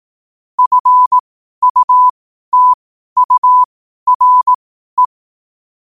Morse code, heard as a keyed tone, about 18 wpm, spaced as for 12 wpm, 1000 Hz. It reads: FUTURE
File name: qotdfiles / 12wpm